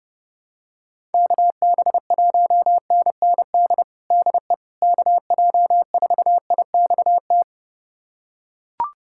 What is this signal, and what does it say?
Morse code, heard as a keyed tone, about 32 wpm, 700 Hz. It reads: KB1NND DE KJ4IXT
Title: USA Callsign DE Callsign - 6 Characters - 30wpm